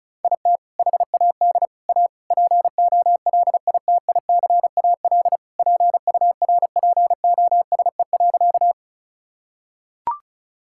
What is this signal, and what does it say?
Morse code, heard as a keyed tone, about 36 wpm, 690 Hz. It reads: IT HAD A POLITICAL PURPOSE.
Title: Sentences from Top 1000 Words - Review Part 5 - 35wpm